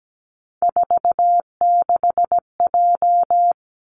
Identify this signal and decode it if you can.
Morse code, heard as a keyed tone, about 17 wpm, 700 Hz. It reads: 46J
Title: Three Challenging Letter-Number Combinations 17wpm